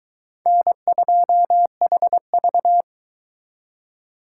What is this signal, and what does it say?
Morse code, heard as a keyed tone, about 22 wpm, 700 Hz. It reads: N2HV